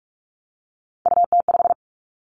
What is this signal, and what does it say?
Morse code, heard as a keyed tone, about 46 wpm, 700 Hz. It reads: UT5
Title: Callsign Prefixes for Top 25 DXCC - 45wpm